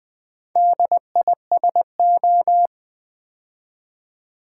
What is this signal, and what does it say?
Morse code, heard as a keyed tone, about 20 wpm, 700 Hz. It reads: DISO
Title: Build A Word - 2 Letters At A Time - 8-Letter Words - 20wpm